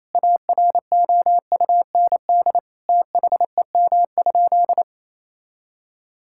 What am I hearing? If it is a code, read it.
Morse code, heard as a keyed tone, about 28 wpm, 700 Hz. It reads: AROUND THEM?
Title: Sentences from Top 1000 Words - Review Part 5 - 28wpm